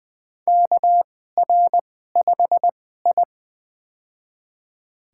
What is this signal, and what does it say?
Morse code, heard as a keyed tone, about 20 wpm, spaced as for 14 wpm, 700 Hz. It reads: KR5I
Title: USA Call Signs - 2x Character Spacing - 20wpm